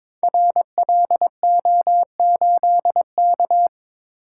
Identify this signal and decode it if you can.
Morse code, heard as a keyed tone, about 22 wpm, 690 Hz. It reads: RLO8K
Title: Code Groups - 5 Letter:Numbers - 22wpm